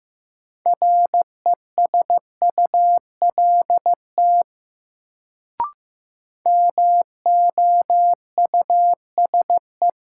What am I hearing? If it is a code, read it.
Morse code, heard as a keyed tone, about 15 wpm, 700 Hz. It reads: RESULT MOUSE
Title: Sentences from Top 700 Words - New Word Drill - 15wpm